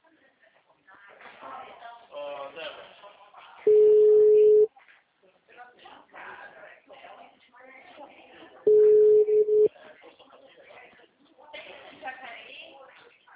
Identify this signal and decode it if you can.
Ringback tone (Europe, 425 Hz), with speech.